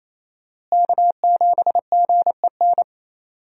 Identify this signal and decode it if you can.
Morse code, heard as a keyed tone, about 28 wpm, 700 Hz. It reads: K7GEN
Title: USA Call Signs - 5 Characters - Repeated 3x - 28wpm